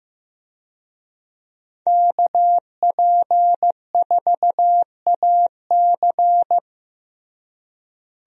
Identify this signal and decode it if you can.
Morse code, heard as a keyed tone, about 15 wpm, 700 Hz. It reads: KP4AC
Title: USA Call Signs - 5 Characters - Repeated 3x - 15wpm